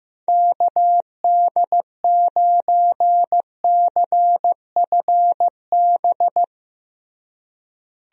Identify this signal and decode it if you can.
Morse code, heard as a keyed tone, about 15 wpm, 700 Hz. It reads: KD9CFB
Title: USA Call Signs - 6 Characters - Repeated 2x - 15wpm